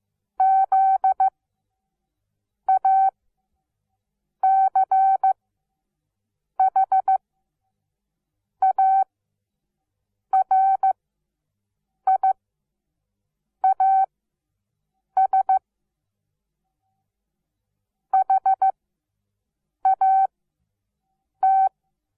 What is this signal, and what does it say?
Morse code, heard as a keyed tone, about 15 wpm, spaced as for 6 wpm, 770 Hz. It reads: ZACHARIAS HAT
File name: Onlinelehrgang / morsen / DJ4UF Morsen17